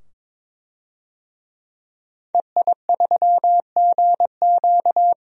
Morse code, heard as a keyed tone, about 22 wpm, 700 Hz. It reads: EI3GQ
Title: DXCC-100 Call Signs - 5 Characters - 22wpm